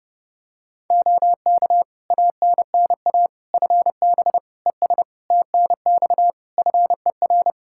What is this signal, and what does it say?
Morse code, heard as a keyed tone, about 30 wpm, 700 Hz. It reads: OK ANNA FB ES TNX FER
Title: Practice QSO - Exchange 2 - Info - 30wpm